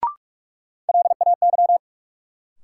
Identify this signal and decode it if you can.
Morse code, heard as a keyed tone, about 46 wpm, 700 Hz. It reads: LAY
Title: Sentences from Top 400 Words - New Word Drill - 45wpm